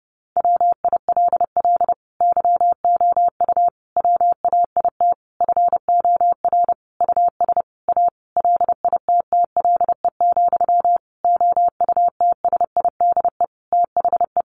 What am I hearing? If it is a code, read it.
Morse code, heard as a keyed tone, about 30 wpm, 700 Hz. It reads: WILL YOU WAIT FOR US A LITTLE, OUTSIDE THE
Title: Sentences from Top 900 Words 30wpm